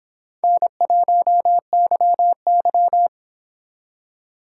Morse code, heard as a keyed tone, about 25 wpm, 700 Hz. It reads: N1YY